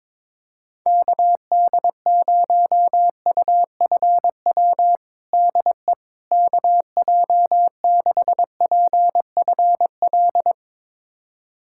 Morse code, heard as a keyed tone, about 21 wpm, 700 Hz. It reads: KD0UFW DE KJ6PFL
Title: USA Callsign DE Callsign - 6 Characters - 22wpm